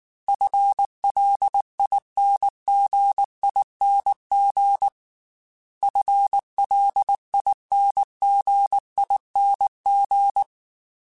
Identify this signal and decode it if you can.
Morse code, heard as a keyed tone, about 19 wpm, 800 Hz. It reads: FLINGING FLINGING
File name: FL / FL mp3s / FL Prefixes 3x2 @18